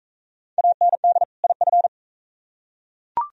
Morse code, heard as a keyed tone, about 41 wpm, 690 Hz. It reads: AND IF